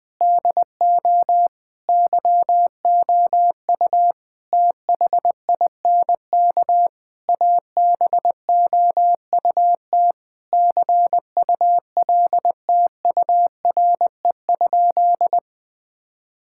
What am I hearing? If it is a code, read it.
Morse code, heard as a keyed tone, about 21 wpm, 700 Hz. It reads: DO YOU THINK ABOUT CULTURE?
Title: Sentences from Top 1000 Words - Review Part 6 - 20wpm